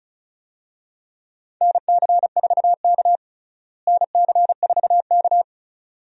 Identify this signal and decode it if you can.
Morse code, heard as a keyed tone, about 36 wpm, 700 Hz. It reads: NC4K NC4K
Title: USA Call Signs - 4 Characters - Repeated 2x - 35wpm